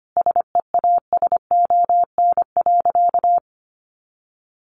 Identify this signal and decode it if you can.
Morse code, heard as a keyed tone, about 25 wpm, 700 Hz. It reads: SEASON.
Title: Sentences from Top 1400 Words 25wpm